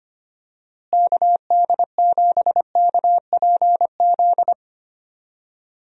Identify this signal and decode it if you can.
Morse code, heard as a keyed tone, about 25 wpm, 700 Hz. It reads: KD7KPZ
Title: USA Call Signs - 6 Characters - Repeated 2x - 25wpm